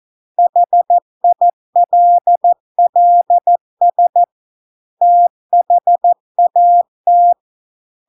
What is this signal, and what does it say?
Morse code, heard as a keyed tone, about 14 wpm, 700 Hz. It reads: HILLS THAT